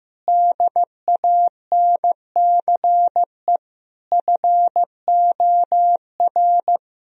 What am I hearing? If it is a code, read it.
Morse code, heard as a keyed tone, about 16 wpm, 700 Hz. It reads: DANCE FOR
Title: Sentences from Top 1000 Words - Review Part 4 - 15wpm